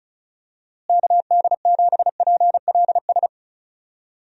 Morse code, heard as a keyed tone, about 34 wpm, 700 Hz. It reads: KD7PLS
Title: USA Call Signs - 6 Characters - Repeated 2x - 35wpm